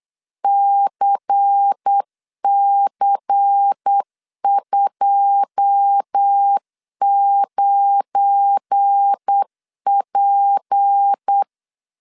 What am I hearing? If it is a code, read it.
Morse code, heard as a keyed tone, about 8 wpm, 800 Hz. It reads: CC29P
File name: LEZIONI / 10 SIGNS / 10C DOT 8wpm 3649